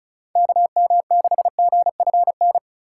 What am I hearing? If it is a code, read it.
Morse code, heard as a keyed tone, about 36 wpm, 700 Hz. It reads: KM6GFN
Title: USA Call Signs - 6 Characters - 2x Character Spacing - 35wpm